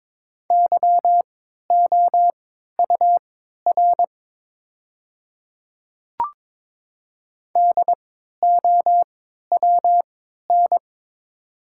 Morse code, heard as a keyed tone, about 22 wpm, spaced as for 13 wpm, 690 Hz. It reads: YOUR DOWN
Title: 3x Character Spacing - Top 100 Words - 22wpm